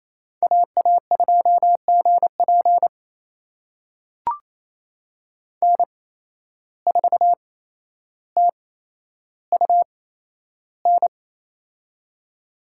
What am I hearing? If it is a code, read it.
Morse code, heard as a keyed tone, about 28 wpm, spaced as for 12 wpm, 700 Hz. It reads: AA2GP N4TUN
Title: USA Call Signs - 5 Characters - 8x Character Spacing - 28wpm